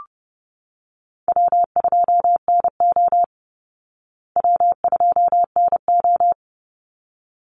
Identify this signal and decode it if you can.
Morse code, heard as a keyed tone, about 30 wpm, 700 Hz. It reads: W2NO W2NO
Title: Call Signs Repeated 3x - 30wpm